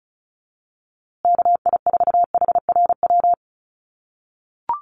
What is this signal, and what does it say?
Morse code, heard as a keyed tone, about 34 wpm, 700 Hz. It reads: KI4HRW
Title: USA Call Signs Repeated 3x 35wpm